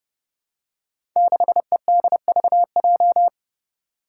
Morse code, heard as a keyed tone, about 30 wpm, 700 Hz. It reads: 6EDVJ
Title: Code Groups - 5 Letter:Numbers - 30wpm